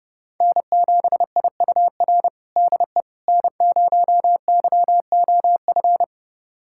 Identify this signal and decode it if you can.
Morse code, heard as a keyed tone, about 32 wpm, 700 Hz. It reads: N7IUR DE N0YOF
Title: USA Callsign DE Callsign - 5 Characters - 30wpm